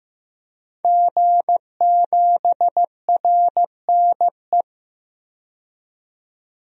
Morse code, heard as a keyed tone, about 15 wpm, 700 Hz. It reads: G7RNE
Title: DXCC-100 Call Signs - 5 Characters - 15wpm